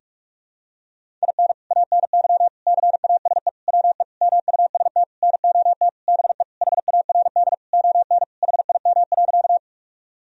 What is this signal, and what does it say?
Morse code, heard as a keyed tone, about 45 wpm, 700 Hz. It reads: IN ANY CASE WE MUST NOT BE HARD ON HIM.
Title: Sentences from Top 1000 Words - Review Part 1 - 45wpm